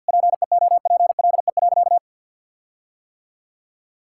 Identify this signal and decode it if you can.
Morse code, heard as a keyed tone, about 50 wpm, 700 Hz. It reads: PEOPLE.
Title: Sentences from Top 1600 Words 50wpm